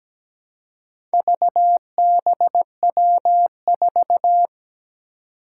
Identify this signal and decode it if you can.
Morse code, heard as a keyed tone, about 17 wpm, 700 Hz. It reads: VBW4
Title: Four Challenging Letter-Number Combinations 17wpm